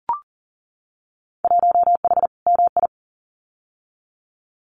Morse code, heard as a keyed tone, about 41 wpm, 700 Hz. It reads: 1H MI
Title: Winter Field Day Exchanges 40wpm